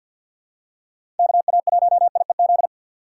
Morse code, heard as a keyed tone, about 47 wpm, 700 Hz. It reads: KA1IEZ
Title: USA Call Signs - 6 Characters - 4x Character Spacing - 50wpm